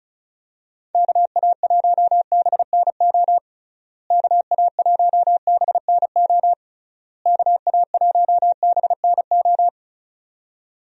Morse code, heard as a keyed tone, about 34 wpm, 700 Hz. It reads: KA1BNO KA1BNO KA1BNO
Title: USA Call Signs - 6 Characters - Repeated 3x - 35wpm